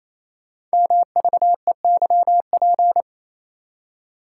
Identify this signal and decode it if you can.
Morse code, heard as a keyed tone, about 29 wpm, 700 Hz. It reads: MVEYP
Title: Code Groups - 5 Letters - 28wpm